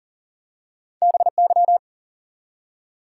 Morse code, heard as a keyed tone, about 39 wpm, 700 Hz. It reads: BY
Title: Top 180 QSO Elements 40wpm